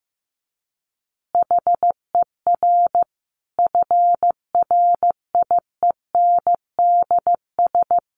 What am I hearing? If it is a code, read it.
Morse code, heard as a keyed tone, about 15 wpm, 700 Hz. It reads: HER FRIENDS
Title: Easy Phrases 15wpm